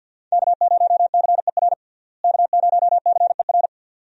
Morse code, heard as a keyed tone, about 50 wpm, 700 Hz. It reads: K0CER K0CER
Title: USA Call Signs - 5 Characters - Repeated 2x - 50wpm